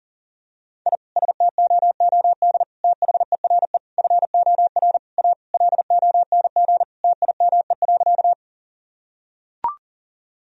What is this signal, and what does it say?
Morse code, heard as a keyed tone, about 37 wpm, 700 Hz. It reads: I STOOD THERE FOR A LONG TIME.
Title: Sentences from Top 1000 Words - Review Part 2 - 40wpm